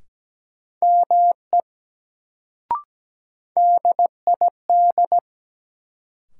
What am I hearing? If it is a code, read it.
Morse code, heard as a keyed tone, about 17 wpm, 700 Hz. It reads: ME DID
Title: Sets of 2 Words - New Word Drill - 17wpm